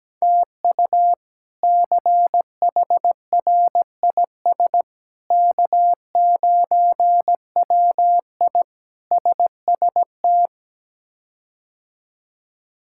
Morse code, heard as a keyed tone, about 17 wpm, 700 Hz. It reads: TU CHRIS K9WI SST
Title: K1USN SST QSO Practice - 17wpm